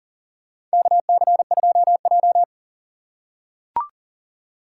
Morse code, heard as a keyed tone, about 39 wpm, 690 Hz. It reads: KC2J
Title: USA Call Signs - 3x Character Spacing - 40wpm